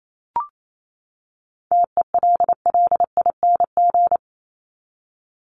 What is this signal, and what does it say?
Morse code, heard as a keyed tone, about 28 wpm, 700 Hz. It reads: TELLING
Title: ING Suffix - Top 100 Words with Base Words 28wpm